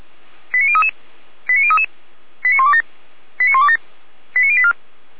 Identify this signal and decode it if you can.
Five-tone selective call (ZVEI), heard as quick runs of five tones, tones of about 80 ms. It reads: 89033 89033 89127 89127 89084
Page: /455/SDR/ZVEI/